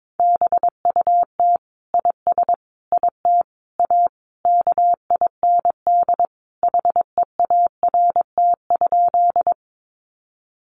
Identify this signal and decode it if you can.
Morse code, heard as a keyed tone, about 22 wpm, 700 Hz. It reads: BUT IS IT A KIND HEART?
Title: Sentences from Top 1200 Words 22wpm